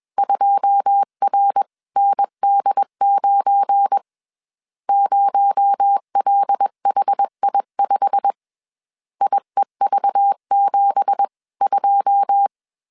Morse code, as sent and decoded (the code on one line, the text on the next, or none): ..--- .-. -. -.. ----. / ----- .-.. .... .. ..... / .. ...- --... ..---
2RND9 0LHI5 IEV72